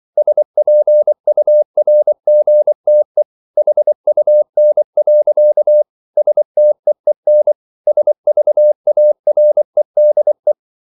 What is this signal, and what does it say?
Morse code, heard as a keyed tone, about 24 wpm, 600 Hz. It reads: SPURGTE HUN. STEEN SVAREDE
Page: Ferie 24 wpm